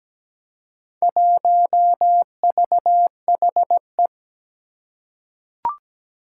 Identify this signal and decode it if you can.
Morse code, heard as a keyed tone, about 17 wpm, 700 Hz. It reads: 1VHE
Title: Four Challenging Letter-Number Combinations 17wpm